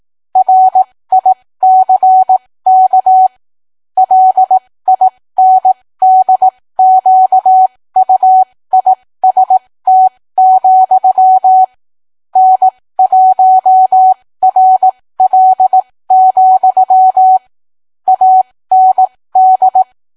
Morse code, heard as a keyed tone, about 18 wpm, spaced as for 15 wpm, 750 Hz. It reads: RICK LINDQUIST, N1RL, AND